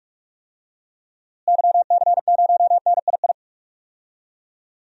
Morse code, heard as a keyed tone, about 46 wpm, 700 Hz. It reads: YC0NII
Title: DXCC-100 Call Signs - 6 Characters - Repeated 2x - 45wpm